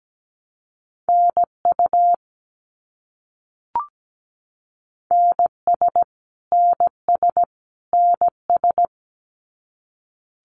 Morse code, heard as a keyed tone, about 17 wpm, 700 Hz. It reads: NU NS NS NS
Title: Canadian Provinces and Territories - Repeated 3x - 17wpm